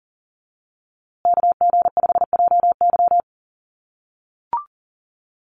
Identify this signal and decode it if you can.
Morse code, heard as a keyed tone, about 40 wpm, 700 Hz. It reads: KG5JY
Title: USA Call Signs Repeated 3x 40wpm